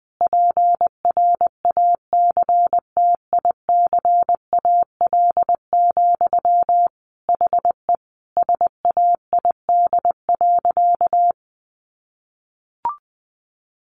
Morse code, heard as a keyed tone, about 20 wpm, 700 Hz. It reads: PRACTICAL, HE SAID.
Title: Sentences from Top 1400 Words 20wpm